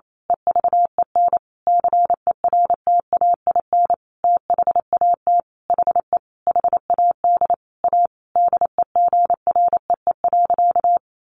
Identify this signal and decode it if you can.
Morse code, heard as a keyed tone, about 28 wpm, 700 Hz. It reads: EVEN CERTAIN THAT HE HAD A DEGREE.
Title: Sentences from Top 1500 Words 28wpm